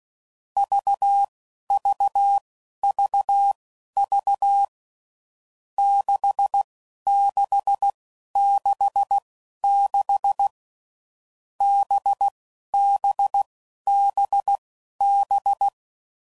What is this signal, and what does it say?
Morse code, heard as a keyed tone, about 16 wpm, spaced as for 11 wpm, 800 Hz. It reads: VVVV 6666 BBBB
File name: Groups / Basic Letter Groups / VB46